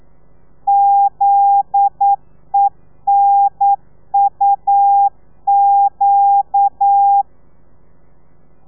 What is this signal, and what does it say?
Morse code, heard as a keyed tone, about 9 wpm, 800 Hz. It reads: ZENUQ
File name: telegrafia / i2tfj / Lez9 40